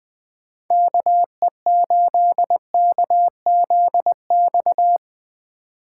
Morse code, heard as a keyed tone, about 23 wpm, 700 Hz. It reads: KE8KZX
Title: Speed Racing - USA 2x3 Call Signs 20wpm